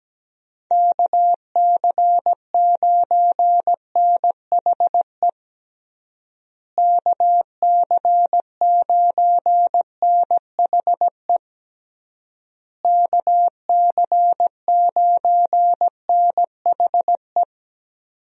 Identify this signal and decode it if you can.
Morse code, heard as a keyed tone, about 17 wpm, 700 Hz. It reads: KC9NHE KC9NHE KC9NHE
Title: USA Call Signs - 6 Characters - Repeated 3x - 17wpm